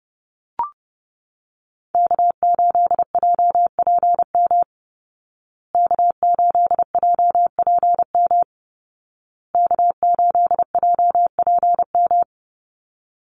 Morse code, heard as a keyed tone, about 30 wpm, 700 Hz. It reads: K8JPM K8JPM K8JPM
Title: Call Signs Repeated 2x - 30wpm